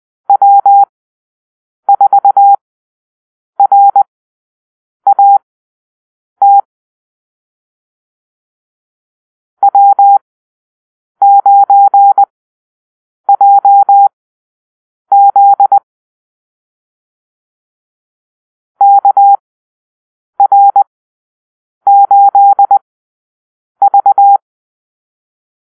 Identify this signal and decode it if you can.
Morse code, heard as a keyed tone, about 20 wpm, spaced as for 7 wpm, 800 Hz. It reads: W4RAT W9JZ KR8V
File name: arrl / Call-Sign Drill-9